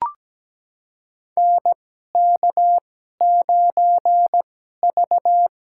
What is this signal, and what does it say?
Morse code, heard as a keyed tone, about 17 wpm, spaced as for 12 wpm, 700 Hz. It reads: NK9V
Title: USA Call Signs - 4 Characters - 2x Character Spacing - 17wpm